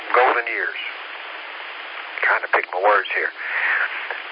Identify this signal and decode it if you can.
Emergency Alert System (SAME) tones, heard as harsh buzzing bursts, with speech. It is heard from 0.4 to 4.1 s.